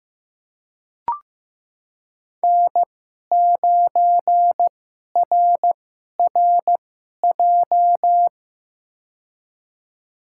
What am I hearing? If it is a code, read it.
Morse code, heard as a keyed tone, about 15 wpm, spaced as for 11 wpm, 700 Hz. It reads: N9RRJ
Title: USA Call Signs - 5 Characters - 2x Character Spacing - 15wpm